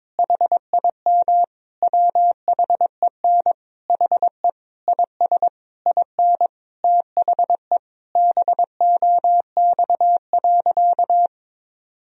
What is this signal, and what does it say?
Morse code, heard as a keyed tone, about 22 wpm, 700 Hz. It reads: HIM WHEN HE IS IN THE BOX.
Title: Sentences from Top 1000 Words - Review Part 2 - 22wpm